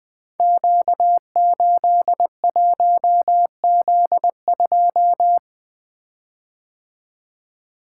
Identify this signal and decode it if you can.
Morse code, heard as a keyed tone, about 20 wpm, 700 Hz. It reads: Q81Z2
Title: Code Groups - 5 Letter:Numbers - 20wpm